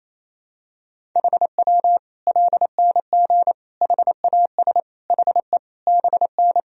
Morse code, heard as a keyed tone, about 28 wpm, 700 Hz. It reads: HW LNG HAS HE BN
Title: Sentences from Top 100 Words and Abbreviations 28wpm